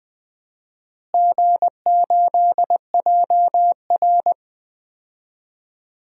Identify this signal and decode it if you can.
Morse code, heard as a keyed tone, about 20 wpm, 700 Hz. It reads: G8JR
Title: DXCC-100 Call Signs - 4 Characters - Repeated 2x - 20wpm